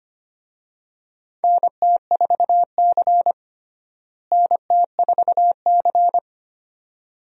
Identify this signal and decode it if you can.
Morse code, heard as a keyed tone, about 25 wpm, 700 Hz. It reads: NT4C NT4C
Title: USA Call Signs - 4 Characters - Repeated 2x - 25wpm